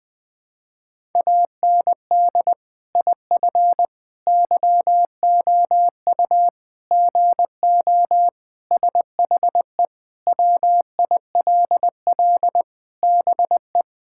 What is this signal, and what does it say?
Morse code, heard as a keyed tone, about 20 wpm, 700 Hz. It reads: AND IF YOU GO SHE WILL BE
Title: Sentences with 3-Word Phrases - Spoken Beginning - 20wpm